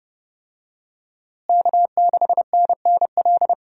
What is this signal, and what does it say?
Morse code, heard as a keyed tone, about 31 wpm, 700 Hz. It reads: K6NNL
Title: USA Call Signs - 5 Characters - Repeated 3x - 30wpm